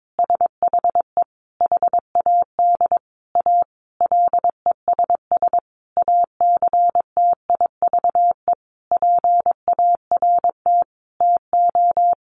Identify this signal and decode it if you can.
Morse code, heard as a keyed tone, about 22 wpm, 700 Hz. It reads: SHE HAD A LESS ACTIVE PART TO
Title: Sentences from Top 900 Words 22wpm